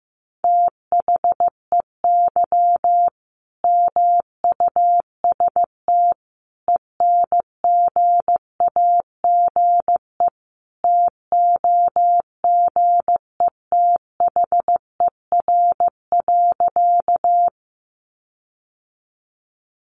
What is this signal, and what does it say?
Morse code, heard as a keyed tone, about 15 wpm, 700 Hz. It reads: THEY MUST ENGAGE TOGETHER.
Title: Sentences from Top 1500 Words 15wpm